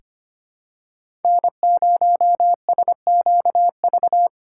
Morse code, heard as a keyed tone, about 25 wpm, 700 Hz. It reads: N0SQV
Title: USA Call Signs - 2x Character Spacing - 25wpm